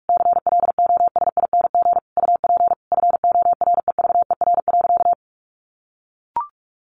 Morse code, heard as a keyed tone, about 44 wpm, 700 Hz. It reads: CLOSING UP FOREVER.
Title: A Sound Of Thunder 45wpm